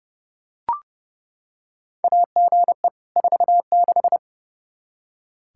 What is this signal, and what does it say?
Morse code, heard as a keyed tone, about 32 wpm, 700 Hz. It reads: AGE 46
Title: Practice QSO - Exchange 3 - More Info - 30wpm